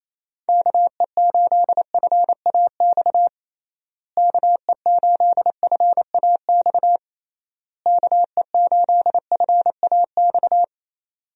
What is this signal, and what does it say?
Morse code, heard as a keyed tone, about 27 wpm, 700 Hz. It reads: KE8FAX KE8FAX KE8FAX
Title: USA Call Signs - 6 Characters - Repeated 3x - 28wpm